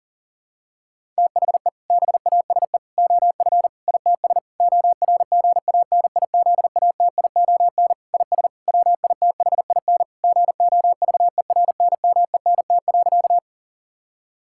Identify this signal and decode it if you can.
Morse code, heard as a keyed tone, about 37 wpm, 700 Hz. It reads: THE BASE OF ITS ORGANIZATION IS WITHIN GOVERNMENT.
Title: Sentences from Top 1000 Words - Review Part 5 - 40wpm